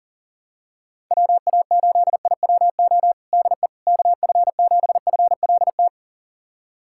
Morse code, heard as a keyed tone, about 40 wpm, 700 Hz. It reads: WA8IWO DE KF7FLT
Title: USA Callsign DE Callsign 40wpm